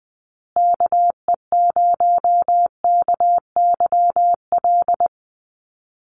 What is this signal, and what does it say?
Morse code, heard as a keyed tone, about 20 wpm, 700 Hz. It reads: KE0KYL
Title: USA Call Signs Repeated 3x 20wpm